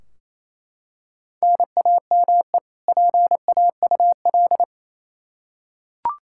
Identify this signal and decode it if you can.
Morse code, heard as a keyed tone, about 28 wpm, 700 Hz. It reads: NAME PAUL
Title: Practice QSO - Exchange 1 - Beginning - 28wpm